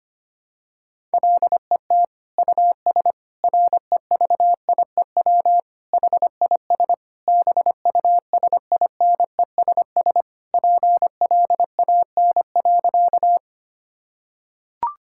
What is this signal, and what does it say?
Morse code, heard as a keyed tone, about 25 wpm, 700 Hz. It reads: LET US REVIEW HIS BUSINESS PLAN.
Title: Sentences from Top 1000 Words - Review Part 6 - 25wpm